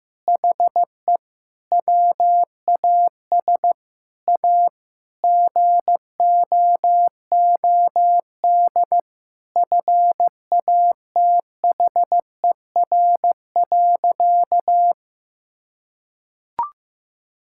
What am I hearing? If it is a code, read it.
Morse code, heard as a keyed tone, about 15 wpm, 700 Hz. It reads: HE WAS A GOOD FATHER.
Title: Sentences from Top 1000 Words - Review Part 5 - 15wpm